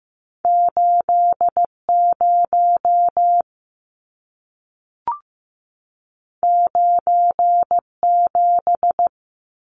Morse code, heard as a keyed tone, about 15 wpm, 700 Hz. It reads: 80 97
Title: Numbers - 2-Digit 15wpm